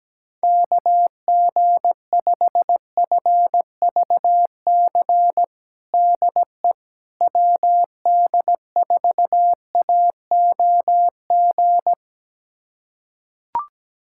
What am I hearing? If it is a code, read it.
Morse code, heard as a keyed tone, about 17 wpm, 700 Hz. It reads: KG5FVC DE WD4AOG
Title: USA Callsign DE Callsign 17wpm